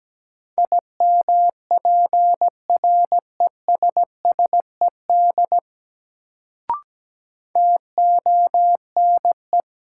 Morse code, heard as a keyed tone, about 17 wpm, 700 Hz. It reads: IMPRESSED TONE
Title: Sentences from Top 1600 Words - New Word Drill Alt - 17wpm